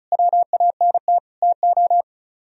Morse code, heard as a keyed tone, about 44 wpm, 700 Hz. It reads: WANT TO